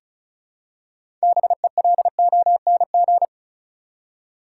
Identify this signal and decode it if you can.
Morse code, heard as a keyed tone, about 36 wpm, 700 Hz. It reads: BELONG
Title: Six Letter Words - 35wpm